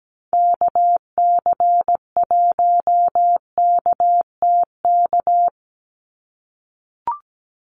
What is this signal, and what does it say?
Morse code, heard as a keyed tone, about 17 wpm, 700 Hz. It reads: KC1KTK
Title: USA Call Signs 17wpm